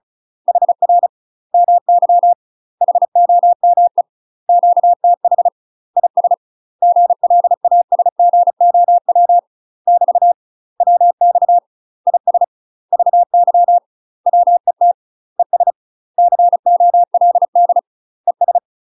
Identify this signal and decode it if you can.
Morse code, heard as a keyed tone, about 34 wpm, 700 Hz. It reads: HR MY HOME QTH IS GLASGOW <BT> WX IS VY WET ES COLD ES
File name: UK MASTER QSOs TRAINING / 35WPM / UK MASTER QSOs 0005